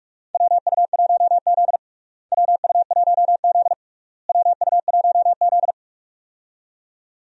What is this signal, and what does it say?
Morse code, heard as a keyed tone, about 42 wpm, 700 Hz. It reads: WU1Z WU1Z WU1Z
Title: USA Call Signs - 4 Characters - Repeated 3x - 45wpm